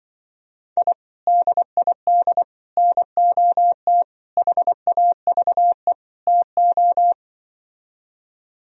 Morse code, heard as a keyed tone, about 24 wpm, 700 Hz. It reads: I DID NOT HAVE TO
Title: Speed-Racing - Sets of 5 Words 20wpm